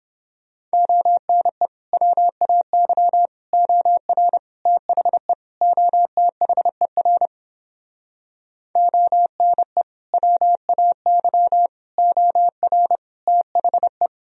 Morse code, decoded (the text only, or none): ONE WAY OR THE OTHER ONE WAY OR THE